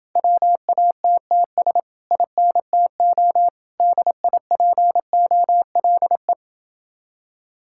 Morse code, heard as a keyed tone, about 30 wpm, 700 Hz. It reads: WATTS INTO DIPOLE